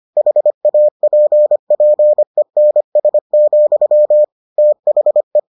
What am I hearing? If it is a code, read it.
Morse code, heard as a keyed tone, about 25 wpm, 600 Hz. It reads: HAPPENS, THE